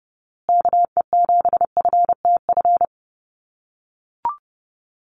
Morse code, heard as a keyed tone, about 30 wpm, 700 Hz. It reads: KE7FTF
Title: USA Call Signs Repeated 2x 30wpm